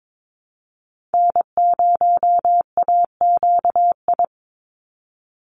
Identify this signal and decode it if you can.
Morse code, heard as a keyed tone, about 22 wpm, 700 Hz. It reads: N0AQI
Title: USA Call Signs 22wpm